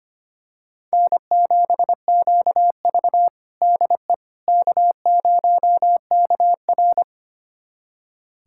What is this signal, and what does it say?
Morse code, heard as a keyed tone, about 25 wpm, 700 Hz. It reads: N7QV DE K0KR
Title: USA Callsign DE Callsign - 4 Characters - 25wpm